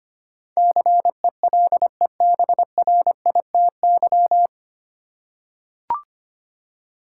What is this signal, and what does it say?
Morse code, heard as a keyed tone, about 25 wpm, 700 Hz. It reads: CELEBRITY
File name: practice / 25wpm / 9 Letter Words - First 3 Letters Given - 25wpm